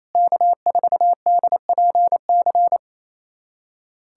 Morse code, heard as a keyed tone, about 28 wpm, 700 Hz. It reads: K4DPC
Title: USA Callsign DE Callsign - 5 Characters - 28wpm